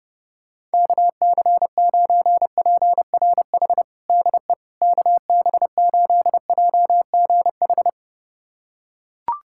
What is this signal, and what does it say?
Morse code, heard as a keyed tone, about 30 wpm, 700 Hz. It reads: KC9PRH DE KB8JGH
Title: USA Callsign DE Callsign - 6 Characters - 30wpm